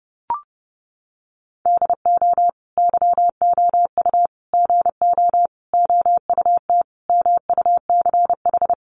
Morse code, heard as a keyed tone, about 30 wpm, 700 Hz. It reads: DO YOU GO OUT MUCH
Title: Sentences from Top 700 Words 30wpm